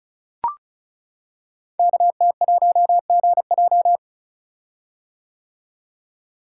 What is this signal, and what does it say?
Morse code, heard as a keyed tone, about 36 wpm, 690 Hz. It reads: KT1GJ
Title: Code Groups - 5 Letter:Numbers - 35wpm